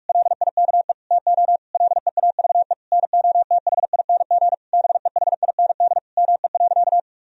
Morse code, heard as a keyed tone, about 46 wpm, 700 Hz. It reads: LIKE TO LEAVE NOTHING BEHIND ME.
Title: Sentences from Top 1000 Words - Review Part 1 - 45wpm